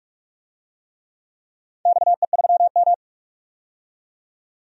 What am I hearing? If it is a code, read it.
Morse code, heard as a keyed tone, about 46 wpm, 700 Hz. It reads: XE3M
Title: DXCC-100 Call Signs - 4 Characters - 45wpm